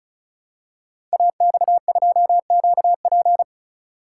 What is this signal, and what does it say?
Morse code, heard as a keyed tone, about 36 wpm, 700 Hz. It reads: AX2QP
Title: Code Groups - 5 Letter:Numbers - 35wpm